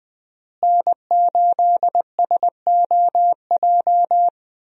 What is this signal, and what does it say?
Morse code, heard as a keyed tone, about 20 wpm, 700 Hz. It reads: N8SOJ
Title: USA Call Signs - 5 Characters - Repeated 2x - 20wpm